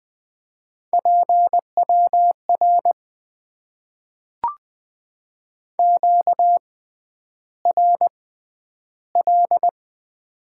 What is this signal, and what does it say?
Morse code, heard as a keyed tone, about 20 wpm, spaced as for 10 wpm, 700 Hz. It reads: PWR QRL
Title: Top 90 QSO Elements - 6x Character Spacing - 20wpm